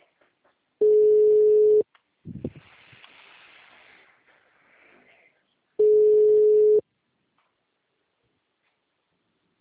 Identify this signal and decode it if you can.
Ringback tone (Europe, 425 Hz), with speech.
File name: out- -2010- - - .2412